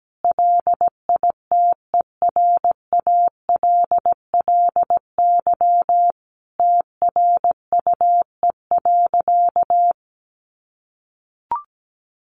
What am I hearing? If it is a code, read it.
Morse code, heard as a keyed tone, about 17 wpm, 700 Hz. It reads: LITERALLY TRUE.
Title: Sentences from Top 1400 Words 17wpm